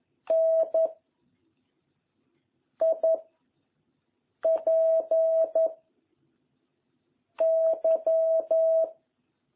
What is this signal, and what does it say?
Morse code, heard as a keyed tone, about 11 wpm, 660 Hz. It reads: N I P Y